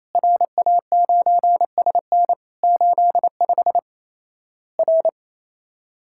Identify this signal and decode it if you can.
Morse code, heard as a keyed tone, about 28 wpm, 700 Hz. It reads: RA9SN 85 R
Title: Worked All Europe DX Contest - QTC Exchange - 28wpm